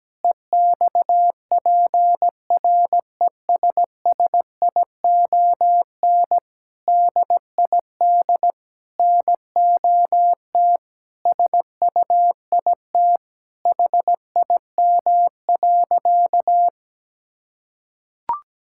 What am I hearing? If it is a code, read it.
Morse code, heard as a keyed tone, about 17 wpm, 700 Hz. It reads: EXPRESSION DID NOT SUIT HIM.
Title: Sentences from Top 1600 Words 17wpm